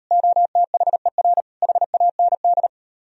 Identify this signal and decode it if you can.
Morse code, heard as a keyed tone, about 38 wpm, 700 Hz. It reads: OTHER HAND